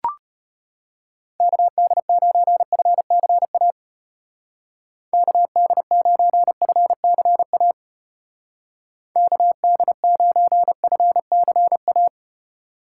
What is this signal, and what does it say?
Morse code, heard as a keyed tone, about 32 wpm, 700 Hz. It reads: KD9FCA KD9FCA KD9FCA